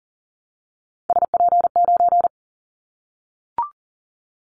Morse code, heard as a keyed tone, about 40 wpm, 700 Hz. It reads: SP9
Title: Callsign Prefixes for Top 25 DXCC - 40wpm